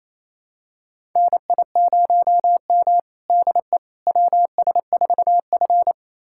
Morse code, heard as a keyed tone, about 28 wpm, 700 Hz. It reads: NI0M DE WS4F
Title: USA Callsign DE Callsign - 4 Characters - 28wpm